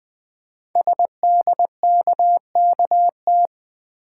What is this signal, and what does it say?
Morse code, heard as a keyed tone, about 20 wpm, 700 Hz. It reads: SDKKT
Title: Code Groups - 5 Letters - 20wpm